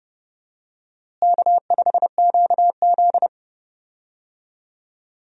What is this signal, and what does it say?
Morse code, heard as a keyed tone, about 30 wpm, 700 Hz. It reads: K5QZ
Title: USA Call Signs - 4 Characters - Repeated 2x - 30wpm